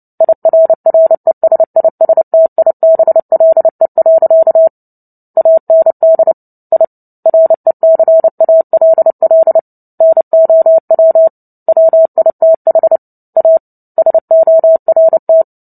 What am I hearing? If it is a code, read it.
Morse code, heard as a keyed tone, about 30 wpm, 650 Hz. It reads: IRRESISTIBLE. AND I RECALL NOW WITH A SORT
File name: Book Two / 03-Chapter 3 - 29 WPM